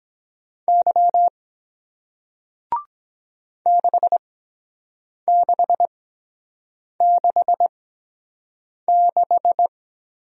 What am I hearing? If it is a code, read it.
Morse code, heard as a keyed tone, about 23 wpm, spaced as for 7 wpm, 700 Hz. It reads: Y 6666